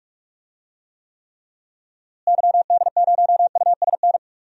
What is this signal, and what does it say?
Morse code, heard as a keyed tone, about 46 wpm, 700 Hz. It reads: YD0USN
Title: DXCC-100 Call Signs - 6 Characters - Repeated 3x - 45wpm